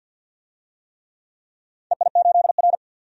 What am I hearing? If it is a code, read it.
Morse code, heard as a keyed tone, about 47 wpm, 700 Hz. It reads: EI8R
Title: DXCC-100 Call Signs - 4 Characters - 50wpm